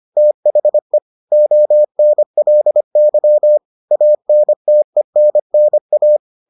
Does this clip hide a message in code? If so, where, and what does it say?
Yes: Morse code from 0.2 to 6.2 s, heard as a keyed tone, about 25 wpm, 600 Hz. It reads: THE ONLY ANTENNA